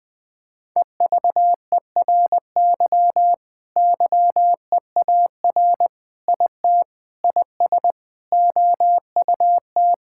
Morse code, heard as a keyed tone, about 20 wpm, 700 Hz. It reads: EVERY YEAR IT IS OUT
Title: Sentences with 9-Word Phrases - 20wpm